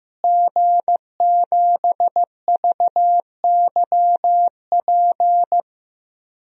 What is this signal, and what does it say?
Morse code, heard as a keyed tone, about 15 wpm, 700 Hz. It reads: G7VYP
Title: DXCC-100 Call Signs - 5 Characters - Repeated 2x - 15wpm